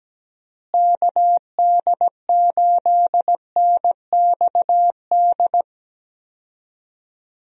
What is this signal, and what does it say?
Morse code, heard as a keyed tone, about 17 wpm, 700 Hz. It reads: KD8NXD
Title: USA Call Signs - 6 Characters - Repeated 3x - 17wpm